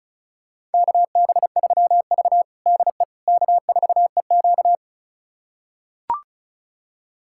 Morse code, heard as a keyed tone, about 36 wpm, 690 Hz. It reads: KB3V DE K4EQ
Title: USA Callsign DE Callsign - 4 Characters - 35wpm